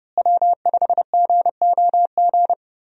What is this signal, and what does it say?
Morse code, heard as a keyed tone, about 32 wpm, 700 Hz. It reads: W5GOG
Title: USA Call Signs - 5 Characters - Repeated 2x - 30wpm